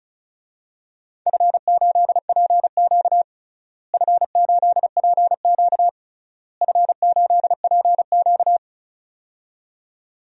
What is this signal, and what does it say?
Morse code, heard as a keyed tone, about 34 wpm, 700 Hz. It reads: F8PQ F8PQ F8PQ
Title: DXCC-100 Call Signs - 4 Characters - Repeated 3x - 35wpm